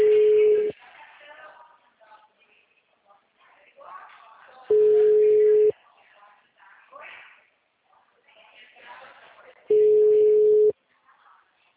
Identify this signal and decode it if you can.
Ringback tone (Europe, 425 Hz), with speech.
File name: out- -2038- - - .226